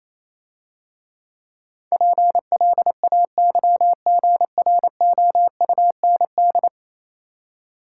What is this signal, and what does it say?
Morse code, heard as a keyed tone, about 27 wpm, 700 Hz. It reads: PLAYGROUND
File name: Easy Words - Speed-Racing with Hints - 22wpm